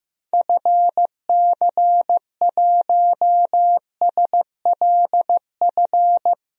Morse code, heard as a keyed tone, about 15 wpm, 700 Hz. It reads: FC1SLF
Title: DXCC-100 Call Signs - 6 Characters - 15wpm